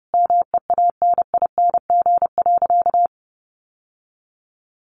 Morse code, heard as a keyed tone, about 30 wpm, 700 Hz. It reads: MEANING.
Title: Sentences from Top 1100 Words 30wpm